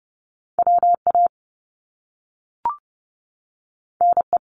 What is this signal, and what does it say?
Morse code, heard as a keyed tone, about 30 wpm, 700 Hz. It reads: WA NE
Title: US States 30wpm